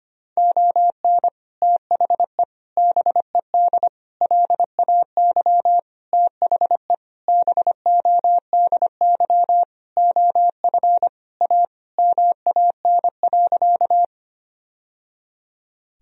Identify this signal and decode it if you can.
Morse code, heard as a keyed tone, about 25 wpm, 700 Hz. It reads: ON THE BED LAY THE BODY OF A MAN.
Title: Sentences from Top 1000 Words - Review Part 2 - 25wpm